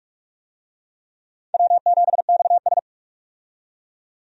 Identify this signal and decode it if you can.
Morse code, heard as a keyed tone, about 46 wpm, 700 Hz. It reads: W7XS
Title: USA Call Signs - 4 Characters - 45wpm